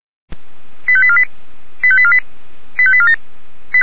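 Five-tone selective call (ZVEI), heard as quick runs of five tones, tones of about 70 ms. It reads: 85848 85848 85847
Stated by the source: ZVEI five-tone selective call received with an SDR.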